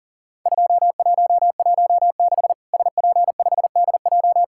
Morse code, heard as a keyed tone, about 39 wpm, 700 Hz. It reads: 2116 SP5DJ
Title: Worked All Europe DX Contest - QTC Exchange - 40wpm